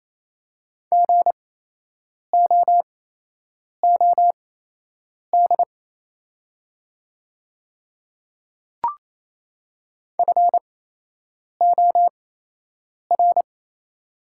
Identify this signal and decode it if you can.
Morse code, heard as a keyed tone, about 28 wpm, spaced as for 8 wpm, 700 Hz. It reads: GOOD FOR
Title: 8x Character Spacing - Top 200 Words - 28wpm